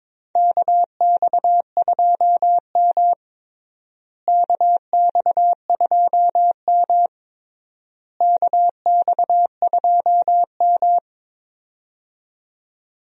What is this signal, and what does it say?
Morse code, heard as a keyed tone, about 22 wpm, 700 Hz. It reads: KX2M KX2M KX2M
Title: USA Call Signs - 4 Characters - Repeated 3x - 22wpm